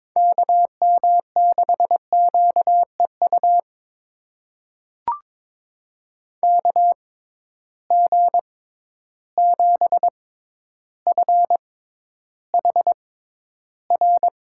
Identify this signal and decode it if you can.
Morse code, heard as a keyed tone, about 22 wpm, spaced as for 11 wpm, 700 Hz. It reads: KM6QEU KG7FHR
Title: USA Call Signs - 6 Characters - 6x Character Spacing - 22wpm